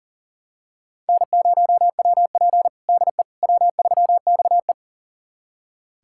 Morse code, heard as a keyed tone, about 40 wpm, 690 Hz. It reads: N0WP DE W3XE
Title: USA Callsign DE Callsign - 4 Characters - 40wpm